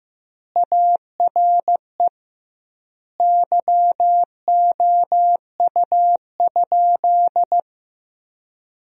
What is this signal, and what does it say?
Morse code, heard as a keyed tone, about 15 wpm, 700 Hz. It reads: ARE YOU?
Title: 2x Word-Spacing - Sentences from Top 100 Words - 15wpm